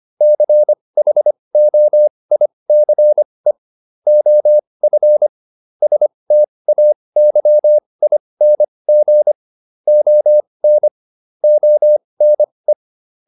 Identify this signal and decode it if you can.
Morse code, heard as a keyed tone, about 25 wpm, spaced as for 20 wpm, 600 Hz. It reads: CHOICE OF STAYING ON ONE